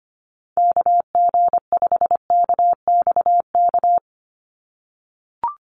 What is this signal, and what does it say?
Morse code, heard as a keyed tone, about 25 wpm, 700 Hz. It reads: KG5KXK
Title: USA Call Signs 25wpm